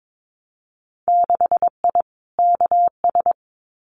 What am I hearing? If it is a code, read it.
Morse code, heard as a keyed tone, about 22 wpm, 700 Hz. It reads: 6I KS
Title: Winter Field Day Exchanges 22wpm